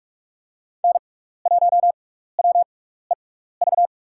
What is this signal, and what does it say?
Morse code, heard as a keyed tone, about 43 wpm, spaced as for 15 wpm, 700 Hz. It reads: N1WEV
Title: USA Call Signs - 5 Characters - 6x Character Spacing - 45wpm